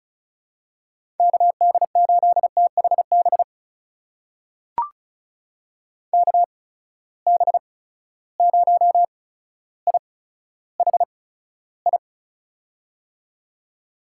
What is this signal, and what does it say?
Morse code, heard as a keyed tone, about 36 wpm, spaced as for 15 wpm, 700 Hz. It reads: KD8THB KB0IHI
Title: USA Call Signs - 6 Characters - 8x Character Spacing - 35wpm